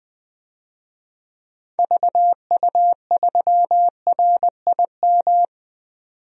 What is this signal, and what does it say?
Morse code, heard as a keyed tone, about 20 wpm, 700 Hz. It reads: VU3RIM
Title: DXCC-100 Call Signs - 6 Characters - 20wpm